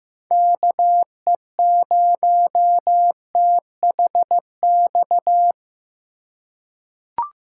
Morse code, heard as a keyed tone, about 15 wpm, 700 Hz. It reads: KE0THX
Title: USA Call Signs - 6 Characters - 3x Character Spacing - 15wpm